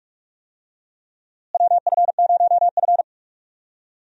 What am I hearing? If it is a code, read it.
Morse code, heard as a keyed tone, about 42 wpm, 700 Hz. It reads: WF0F
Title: USA Call Signs - 4 Characters - Repeated 2x - 45wpm